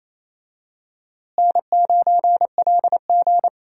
Morse code, heard as a keyed tone, about 28 wpm, 700 Hz. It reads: N9LG
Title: USA Call Signs - 4 Characters - Repeated 2x - 28wpm